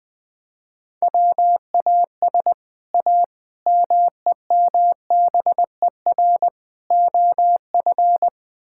Morse code, heard as a keyed tone, about 20 wpm, 700 Hz. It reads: WAS A MEMBER OF
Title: Build A Phrase - 5-Word Phrases - 20wpm